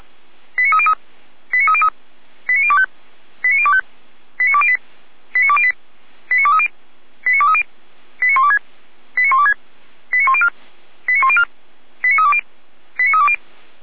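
Five-tone selective call (ZVEI), heard as quick runs of five tones, tones of about 70 ms. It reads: 89303 89303 89025 89025 89208 89208 89230 89230 89126 89126 89104 89104 89320 89320
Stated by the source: ZVEI five-tone call received by SDR.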